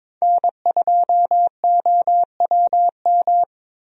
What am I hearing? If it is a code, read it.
Morse code, heard as a keyed tone, about 21 wpm, 700 Hz. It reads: N2OWM
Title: USA Call Signs - 5 Characters - Repeated 3x - 22wpm